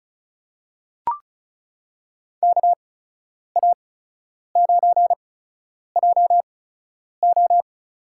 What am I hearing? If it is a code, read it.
Morse code, heard as a keyed tone, about 36 wpm, spaced as for 10 wpm, 700 Hz. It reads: KA9JO
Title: USA Call Signs - 8x Character Spacing - 35wpm